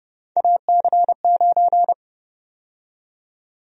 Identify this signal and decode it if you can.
Morse code, heard as a keyed tone, about 30 wpm, 700 Hz. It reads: AC9
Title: USA Call Sign Prefixes - 30wpm